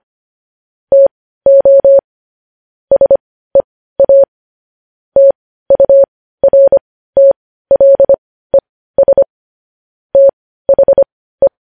Morse code, heard as a keyed tone, about 25 wpm, spaced as for 15 wpm, 550 Hz. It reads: TO SEA TURTLES THE
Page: News Headlines in Morse Code at 15 WPM